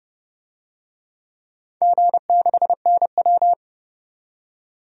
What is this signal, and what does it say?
Morse code, heard as a keyed tone, about 32 wpm, 700 Hz. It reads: G6NW
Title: DXCC-100 Call Signs - 4 Characters - 30wpm